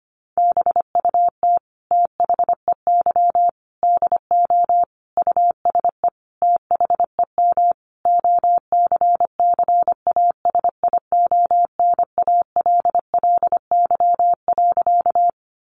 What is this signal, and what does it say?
Morse code, heard as a keyed tone, about 25 wpm, 700 Hz. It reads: BUT THEY DO USE THEM OCCASIONALLY.
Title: Sentences from Top 1300 Words 25wpm